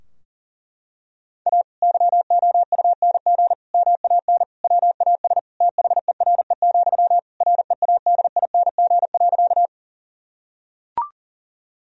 Morse code, heard as a keyed tone, about 40 wpm, 700 Hz. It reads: A YOUNG MAN WAS THERE, READING.
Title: Sentences from Top 1000 Words - Review Part 3 - 40wpm